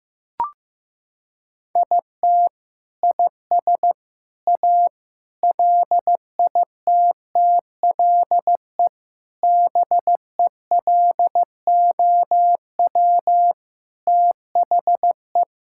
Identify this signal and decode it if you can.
Morse code, heard as a keyed tone, about 15 wpm, 700 Hz. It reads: IT IS A LITTLE BELOW THE
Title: Sentences from Top 1000 Words - Review Part 5 - 15wpm